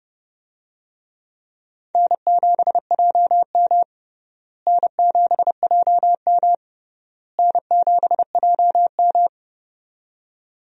Morse code, heard as a keyed tone, about 32 wpm, 700 Hz. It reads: N7JM N7JM N7JM
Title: USA Call Signs - 4 Characters - Repeated 3x - 30wpm